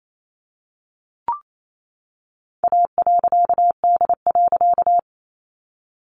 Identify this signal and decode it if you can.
Morse code, heard as a keyed tone, about 28 wpm, 700 Hz. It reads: A.D.
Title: A Sound Of Thunder 28wpm